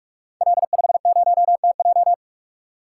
Morse code, heard as a keyed tone, about 44 wpm, 700 Hz. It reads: L50TJ
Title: Code Groups - 5 Letter:Numbers - 45wpm